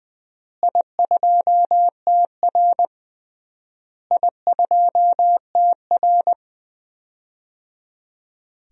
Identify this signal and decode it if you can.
Morse code, heard as a keyed tone, about 20 wpm, 700 Hz. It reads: I2TR I2TR
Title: DXCC-100 Call Signs - 4 Characters - Repeated 3x - 20wpm